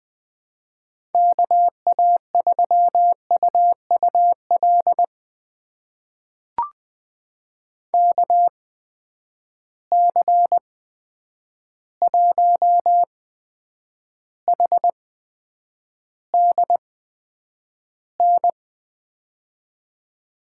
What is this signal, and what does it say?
Morse code, heard as a keyed tone, about 20 wpm, spaced as for 9 wpm, 700 Hz. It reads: KA3UUL KC1HDN
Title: USA Call Signs - 8x Character Spacing - 20wpm